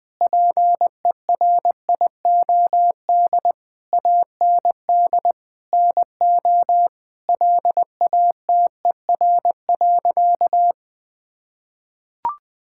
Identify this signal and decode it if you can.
Morse code, heard as a keyed tone, about 20 wpm, 700 Hz. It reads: PERIOD AND NO LATER.
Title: Sentences from Top 1000 Words - Review Part 5 - 20wpm